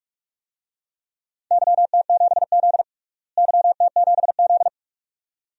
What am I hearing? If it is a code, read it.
Morse code, heard as a keyed tone, about 44 wpm, 700 Hz. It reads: YT7Z YT7Z
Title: DXCC-100 Call Signs - 4 Characters - Repeated 2x - 45wpm